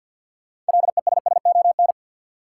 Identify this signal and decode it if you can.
Morse code, heard as a keyed tone, about 52 wpm, 700 Hz. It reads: LESSON
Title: Sentences from Top 1200 Words - New Word Drill - 50wpm